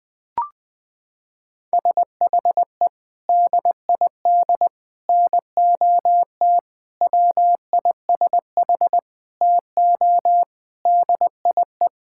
Morse code, heard as a keyed tone, about 20 wpm, 700 Hz. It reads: SHE DID NOT WISH TO DIE
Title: Sentences from Top 1600 Words 20wpm